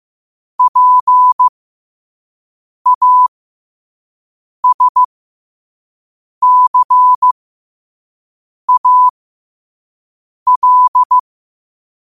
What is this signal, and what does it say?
Morse code, heard as a keyed tone, about 15 wpm, spaced as for 5 wpm, 1000 Hz. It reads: PASCAL
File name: qotdfiles / 5wpm